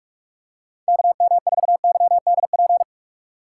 Morse code, heard as a keyed tone, about 46 wpm, 700 Hz. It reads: KM4YDP
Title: USA Call Signs - 4x Character Spacing - 45wpm